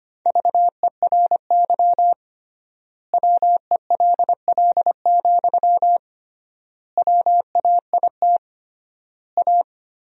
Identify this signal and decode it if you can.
Morse code, heard as a keyed tone, about 25 wpm, 700 Hz. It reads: VERY WELL, WAIT A
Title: 3x Word-Spacing - Sentences from Top 500 Words - 25wpm